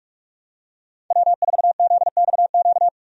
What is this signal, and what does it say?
Morse code, heard as a keyed tone, about 46 wpm, 700 Hz. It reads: W4ZXQ
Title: USA Call Signs - 5 Characters - 45wpm